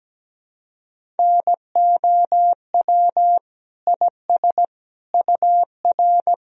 Morse code, heard as a keyed tone, about 17 wpm, 700 Hz. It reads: NOW IS UR
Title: Sentences from Top 100 Words and Abbreviations 17wpm